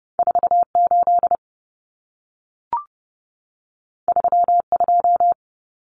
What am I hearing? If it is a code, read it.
Morse code, heard as a keyed tone, about 30 wpm, 700 Hz. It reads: 48 32
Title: Common Cut Numbers - 2-Digit 30wpm